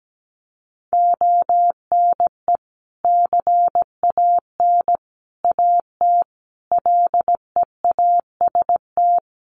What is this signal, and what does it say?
Morse code, heard as a keyed tone, about 17 wpm, 700 Hz. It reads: ONE CAN AT LEAST
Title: Sentences from Top 1200 Words 17wpm